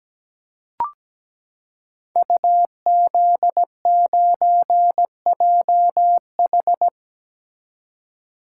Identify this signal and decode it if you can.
Morse code, heard as a keyed tone, about 17 wpm, 700 Hz. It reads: UZ9JH
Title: Code Groups - 5 Letter:Numbers - 17wpm